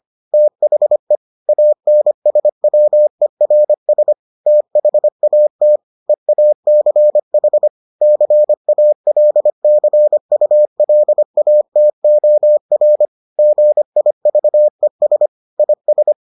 Morse code, heard as a keyed tone, about 25 wpm, 600 Hz. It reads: THE ANSWERS THAT EACH CALCULATOR GIVES IS